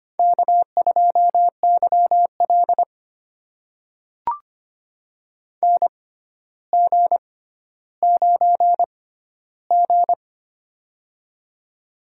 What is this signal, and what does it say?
Morse code, heard as a keyed tone, about 25 wpm, spaced as for 13 wpm, 700 Hz. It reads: K2YL NG9G
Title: USA Call Signs - 4 Characters - 6x Character Spacing - 25wpm